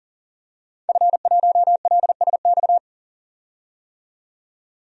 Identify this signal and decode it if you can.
Morse code, heard as a keyed tone, about 40 wpm, 700 Hz. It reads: F1LSX
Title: DXCC-100 Call Signs - 5 Characters - 40wpm